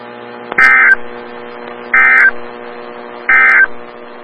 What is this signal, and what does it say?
Emergency Alert System (SAME) tones, heard as harsh buzzing bursts. It is heard from 0.5 to 3.7 s.
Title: Des Moines, IA NOAA Weather Radio - WXL57